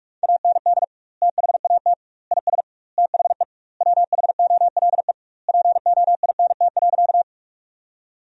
Morse code, heard as a keyed tone, about 46 wpm, 700 Hz. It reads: AND THAT IS THE WHOLE POINT.
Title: 2x Word-Spacing - Sentences from Top 400 Words - 45wpm